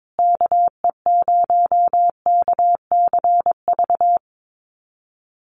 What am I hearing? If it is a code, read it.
Morse code, heard as a keyed tone, about 22 wpm, 700 Hz. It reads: KE0KCV
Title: USA Call Signs Repeated 3x 22wpm